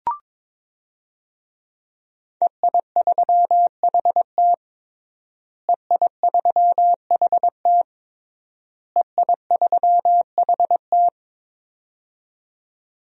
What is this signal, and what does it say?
Morse code, heard as a keyed tone, about 22 wpm, 700 Hz. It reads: EI3HT EI3HT EI3HT
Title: DXCC-100 Call Signs - 5 Characters - Repeated 3x - 22wpm